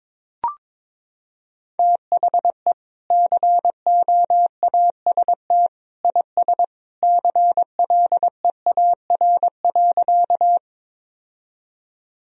Morse code, heard as a keyed tone, about 22 wpm, 700 Hz. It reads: THE COAST IS CLEAR.
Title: Sentences from Top 1000 Words - Review Part 4 - 22wpm